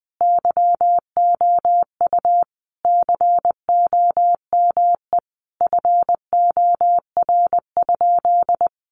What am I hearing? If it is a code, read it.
Morse code, heard as a keyed tone, about 20 wpm, 700 Hz. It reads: YOU COME FOR?
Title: Sentences from Top 100 Words 20wpm